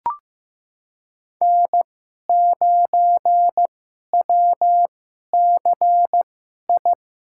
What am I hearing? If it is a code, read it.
Morse code, heard as a keyed tone, about 15 wpm, spaced as for 11 wpm, 700 Hz. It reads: N9WCI
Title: USA Call Signs - 5 Characters - 2x Character Spacing - 15wpm